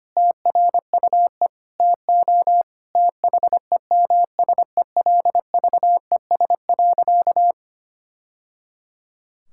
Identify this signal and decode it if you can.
Morse code, heard as a keyed tone, about 25 wpm, 700 Hz. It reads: TRUE TO THEMSELVES.
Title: Sentences from Top 1000 Words - Review Part 6 - 25wpm